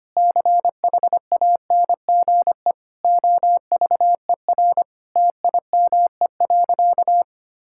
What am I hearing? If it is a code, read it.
Morse code, heard as a keyed tone, about 25 wpm, 700 Hz. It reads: CHANGE OVER TIME.
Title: Sentences from Top 1000 Words - Review Part 6 - 25wpm